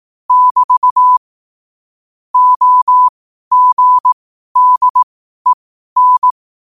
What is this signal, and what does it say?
Morse code, heard as a keyed tone, about 18 wpm, spaced as for 12 wpm, 1000 Hz. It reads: <BT> OGDEN
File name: qotdfiles / 12wpm